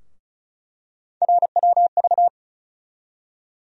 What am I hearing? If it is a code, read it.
Morse code, heard as a keyed tone, about 36 wpm, 700 Hz. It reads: RWV
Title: USA Call Sign Postfixes - 3 Letters 35wpm